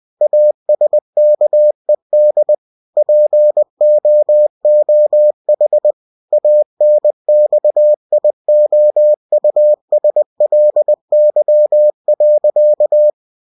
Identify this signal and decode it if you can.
Morse code, heard as a keyed tone, about 20 wpm, 600 Hz. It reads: ASKED POOH ANXIOUSLY.